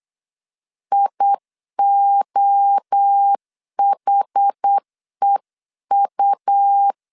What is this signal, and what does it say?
Morse code, heard as a keyed tone, about 8 wpm, 800 Hz. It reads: IOHEU